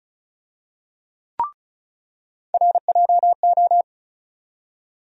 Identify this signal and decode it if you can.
Morse code, heard as a keyed tone, about 34 wpm, 700 Hz. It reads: RJO
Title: USA Call Sign Postfixes - 3 Letters 35wpm